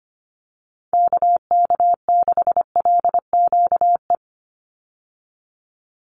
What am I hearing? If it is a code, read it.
Morse code, heard as a keyed tone, about 25 wpm, 700 Hz. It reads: KK6LQE
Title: USA Call Signs 25wpm